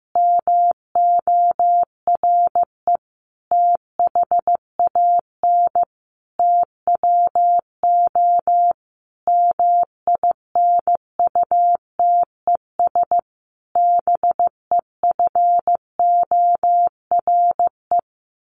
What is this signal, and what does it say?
Morse code, heard as a keyed tone, about 15 wpm, 700 Hz. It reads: MORE THAN TWO MINUTES BEFORE
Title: A Sound Of Thunder 15wpm